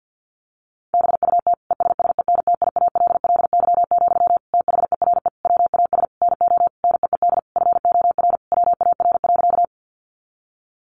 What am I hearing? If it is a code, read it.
Morse code, heard as a keyed tone, about 50 wpm, 700 Hz. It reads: BUT ESSENTIALLY, THERE WAS NO NEED FOR WAR.
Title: Sentences from Top 1300 Words 50wpm